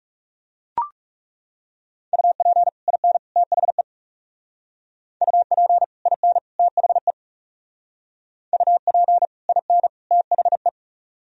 Morse code, heard as a keyed tone, about 39 wpm, 700 Hz. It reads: UP IN THE UP IN THE UP IN THE